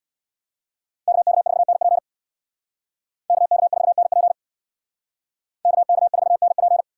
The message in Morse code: KK4NP KK4NP KK4NP

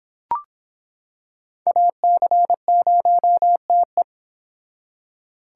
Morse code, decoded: AC0TE

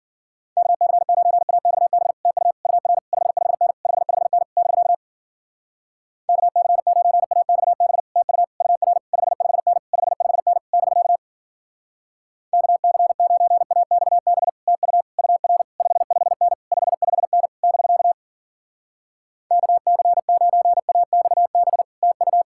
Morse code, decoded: KC9AXB TU UR 55N 55N <BK> KC9AXB TU UR 55N 55N <BK> KC9AXB TU UR 55N 55N <BK> KC9AXB TU

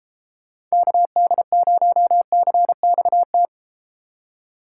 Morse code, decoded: KD0CXT

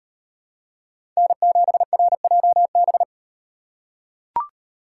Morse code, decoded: N7RJB

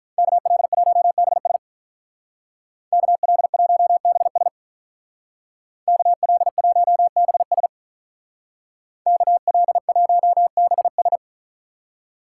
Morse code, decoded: KL1BS KL1BS KL1BS KL1BS

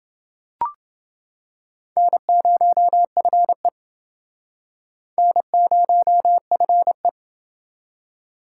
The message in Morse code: N0FE N0FE

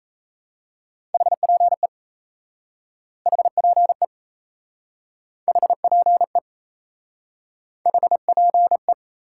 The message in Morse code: HPE HPE HPE HPE

